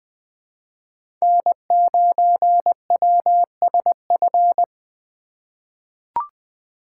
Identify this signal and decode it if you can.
Morse code, heard as a keyed tone, about 20 wpm, 700 Hz. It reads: N9WSF
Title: USA Call Signs - 5 Characters - 4x Character Spacing - 20wpm